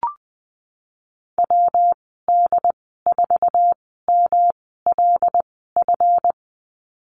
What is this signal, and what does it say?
Morse code, heard as a keyed tone, about 20 wpm, spaced as for 15 wpm, 700 Hz. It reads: WD4MLF
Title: 140 Lesson 23g - Letter f - Call Signs 20wpm